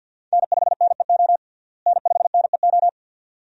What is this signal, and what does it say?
Morse code, heard as a keyed tone, about 51 wpm, 700 Hz. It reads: N5NEO N5NEO
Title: USA Call Signs - 5 Characters - Repeated 2x - 50wpm